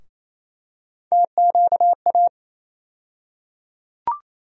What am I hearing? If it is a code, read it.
Morse code, heard as a keyed tone, about 27 wpm, 700 Hz. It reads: TQA
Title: USA Call Sign Postfixes - 3 Letters 28wpm